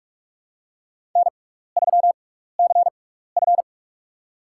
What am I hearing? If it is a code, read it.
Morse code, heard as a keyed tone, about 42 wpm, spaced as for 15 wpm, 700 Hz. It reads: N3CF
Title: USA Call Signs - 4 Characters - 6x Character Spacing - 45wpm